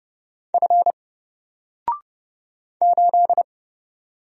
Morse code, heard as a keyed tone, about 30 wpm, 700 Hz. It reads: F 8
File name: Single-Letter-Number-30wpm